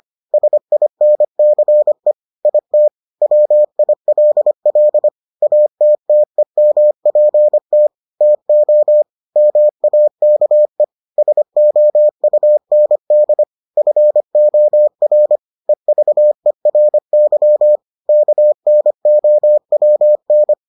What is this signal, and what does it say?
Morse code, heard as a keyed tone, about 25 wpm, 600 Hz. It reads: SINCE IT WILL ATTEMPT TO MAKE SOUND FOR EVERY KNOWN